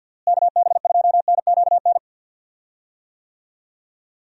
Morse code, heard as a keyed tone, about 51 wpm, 700 Hz. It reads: KB2NQN
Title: USA Call Signs - 6 Characters - Repeated 2x - 50wpm